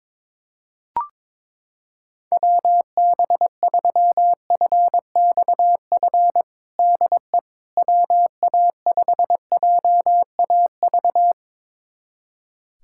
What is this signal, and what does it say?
Morse code, heard as a keyed tone, about 22 wpm, 700 Hz. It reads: WB3FXF DE WA5JAV
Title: USA Callsign DE Callsign 22wpm